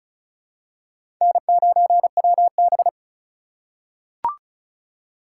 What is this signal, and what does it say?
Morse code, heard as a keyed tone, about 36 wpm, 700 Hz. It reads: N9WB
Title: USA Call Signs - 4 Characters - 2x Character Spacing - 35wpm